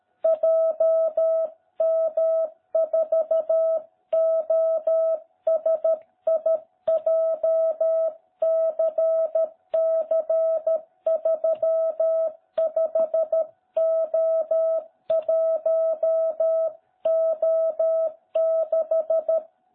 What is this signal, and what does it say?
Morse code, heard as a keyed tone, about 13 wpm, 660 Hz. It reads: JM4OSI JCC35O1O6